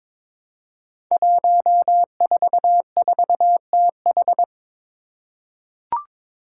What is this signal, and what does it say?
Morse code, heard as a keyed tone, about 23 wpm, 700 Hz. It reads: 144TH
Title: Five Challenging Letter-Number Combinations 22wpm